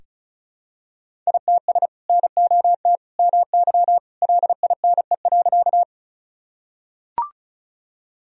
Morse code, heard as a keyed tone, about 36 wpm, 700 Hz. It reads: ITS NOT MY LINE.
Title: Sentences from Top 1000 Words - Review Part 1 - 35wpm